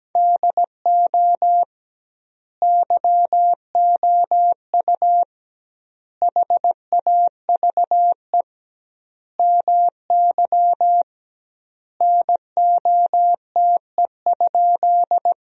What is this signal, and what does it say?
Morse code, heard as a keyed tone, about 17 wpm, 700 Hz. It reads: DO YOU HAVE MY NOTE?
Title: 2x Word-Spacing - Sentences from Top 500 Words - 17wpm